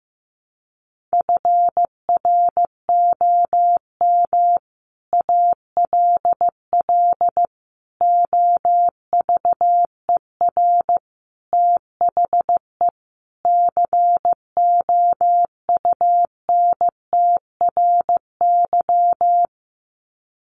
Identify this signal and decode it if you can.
Morse code, heard as a keyed tone, about 15 wpm, 700 Hz. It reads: FROM ALL OVER THE COUNTRY
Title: Sets of 5 Words Encore 15wpm